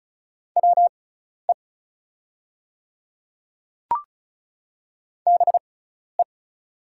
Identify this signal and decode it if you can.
Morse code, heard as a keyed tone, about 34 wpm, spaced as for 12 wpm, 700 Hz. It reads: WE BE
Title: 6x Character Spacing - Top 100 Words - 35wpm